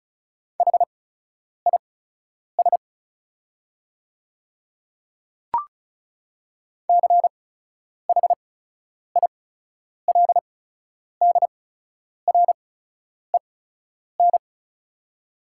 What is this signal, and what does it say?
Morse code, heard as a keyed tone, about 34 wpm, spaced as for 10 wpm, 700 Hz. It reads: HIS CHILDREN
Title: 8x Character Spacing - Top 300 Words - 35wpm